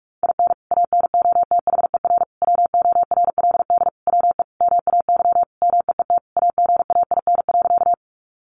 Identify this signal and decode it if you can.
Morse code, heard as a keyed tone, about 44 wpm, 700 Hz. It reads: IN ANOTHER WORLD WE MAY MEET AGAIN.
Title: Sentences from Top 1100 Words 45wpm